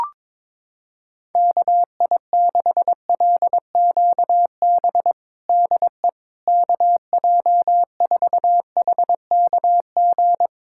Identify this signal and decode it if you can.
Morse code, heard as a keyed tone, about 21 wpm, 700 Hz. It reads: KI6LQB DE KJ4HKG
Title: USA Callsign DE Callsign - 6 Characters - 22wpm